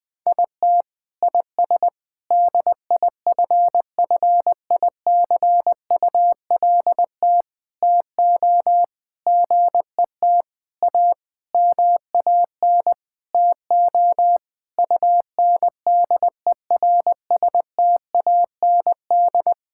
Morse code, heard as a keyed tone, about 20 wpm, 700 Hz. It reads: IT IS DIFFICULT TO GET A MAN TO UNDERSTAND
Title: Sentences with 9-Word Phrases - 20wpm